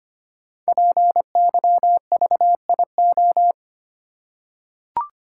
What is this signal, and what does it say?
Morse code, heard as a keyed tone, about 25 wpm, 700 Hz. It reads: PYVIO
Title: Code Groups - 5 Letter:Numbers - 25wpm